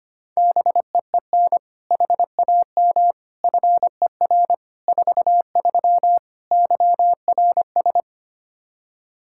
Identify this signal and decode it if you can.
Morse code, heard as a keyed tone, about 25 wpm, 700 Hz. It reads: BEEN HAM FER 43 YRS
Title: Practice QSO - Exchange 3 - More Info - 25wpm